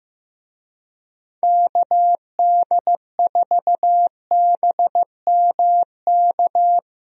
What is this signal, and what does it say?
Morse code, heard as a keyed tone, about 15 wpm, 700 Hz. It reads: KD4BMK
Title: USA Call Signs - 6 Characters - Repeated 3x - 15wpm